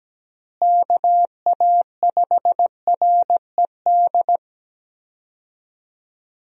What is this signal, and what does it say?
Morse code, heard as a keyed tone, about 17 wpm, 700 Hz. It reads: KA5RED
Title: USA Call Signs - 6 Characters - Repeated 3x - 17wpm